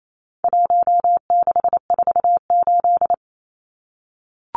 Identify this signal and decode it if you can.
Morse code, heard as a keyed tone, about 28 wpm, 700 Hz. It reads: 1648
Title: Numbers - 4-Digit 28wpm